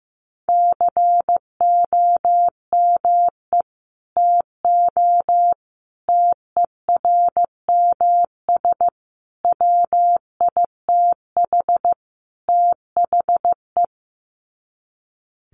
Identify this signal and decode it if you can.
Morse code, heard as a keyed tone, about 15 wpm, 700 Hz. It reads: COME TO TERMS WITH THE
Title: Sets of 5 Words Encore2 15wpm